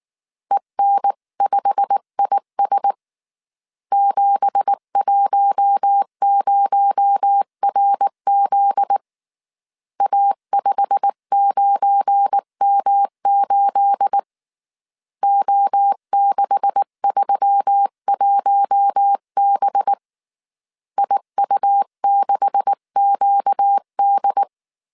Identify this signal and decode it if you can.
Morse code, heard as a keyed tone, about 19 wpm, 800 Hz. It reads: EN5IS 710RZ A59M8 O631B IU6QD
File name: CW / DET / 1 9 1500C S C 18wpm 2539